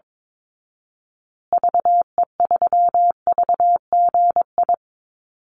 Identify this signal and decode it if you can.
Morse code, heard as a keyed tone, about 22 wpm, 700 Hz. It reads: VE3VGI
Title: Call Signs Repeated 3x - 22wpm